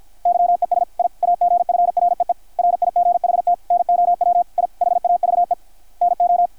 Morse code, heard as a keyed tone, about 53 wpm, 660 Hz, with radio noise. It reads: YES I AGFREE RIGHT NOW I HAVE NO
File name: archived / CRSnet